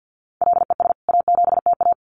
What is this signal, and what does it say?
Morse code, heard as a keyed tone, about 50 wpm, 700 Hz. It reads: LES A7TA
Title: CWT Exchange 50wpm